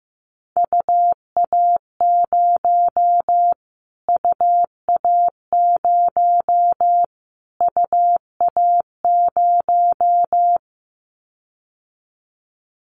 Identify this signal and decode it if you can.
Morse code, heard as a keyed tone, about 15 wpm, 700 Hz. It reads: UA0 UA0 UA0
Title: Callsign Prefixes for Top 25 DXCC - Repeated 3x - 15wpm